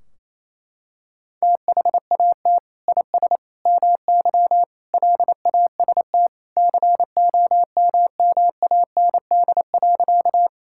Morse code, heard as a keyed tone, about 28 wpm, 700 Hz. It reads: THAT IS MY LAST COMMAND.
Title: Sentences from Top 1600 Words 28wpm